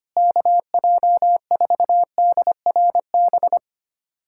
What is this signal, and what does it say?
Morse code, heard as a keyed tone, about 25 wpm, 700 Hz. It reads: KJ4DRB
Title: USA Callsign DE Callsign 25wpm